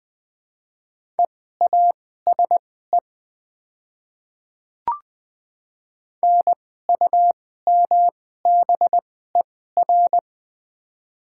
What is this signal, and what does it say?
Morse code, heard as a keyed tone, about 20 wpm, spaced as for 14 wpm, 700 Hz. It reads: EASE NUMBER
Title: 2x Character Spacing - Top 300 Words - 20wpm